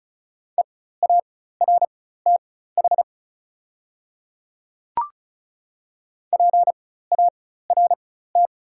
Morse code, heard as a keyed tone, about 34 wpm, spaced as for 16 wpm, 700 Hz. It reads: EARTH PART
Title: 4x Character Spacing - Top 200 Words - 35wpm